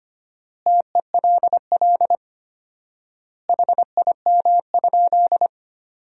Morse code, heard as a keyed tone, about 25 wpm, 700 Hz. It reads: TELL HIM?
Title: 4x Word-Spacing - Sentences from Top 200 Words - 25wpm